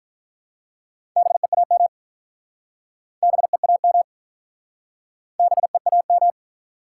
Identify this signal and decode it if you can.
Morse code, heard as a keyed tone, about 44 wpm, 700 Hz. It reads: BEAM BEAM BEAM